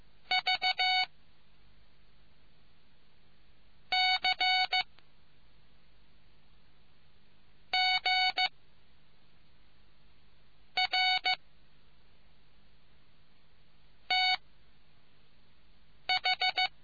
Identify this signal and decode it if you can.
Morse code, heard as a keyed tone, about 15 wpm, 2250 Hz. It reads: V C G R T H